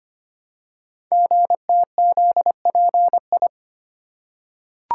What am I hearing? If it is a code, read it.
Morse code, heard as a keyed tone, about 25 wpm, 700 Hz. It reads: GTZPI
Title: Code Groups - 5 Letters - 25wpm